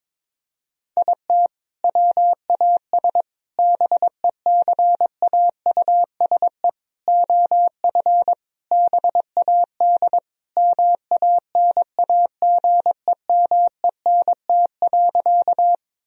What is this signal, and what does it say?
Morse code, heard as a keyed tone, about 22 wpm, 700 Hz. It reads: IT WAS BECAUSE OF BAD MANAGEMENT.
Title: Sentences from Top 1000 Words - Review Part 4 - 22wpm